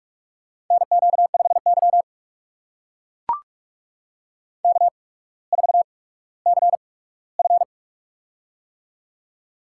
Morse code, heard as a keyed tone, about 44 wpm, spaced as for 19 wpm, 700 Hz. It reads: NQ5Y K4CF
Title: USA Call Signs - 4 Characters - 8x Character Spacing - 45wpm